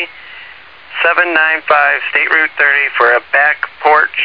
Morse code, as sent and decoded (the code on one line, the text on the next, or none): none